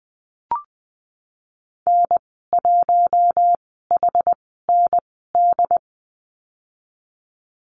Morse code, heard as a keyed tone, about 20 wpm, spaced as for 14 wpm, 700 Hz. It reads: N1HND
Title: 128 Lesson 21g - Number 6 - Call Signs 20wpm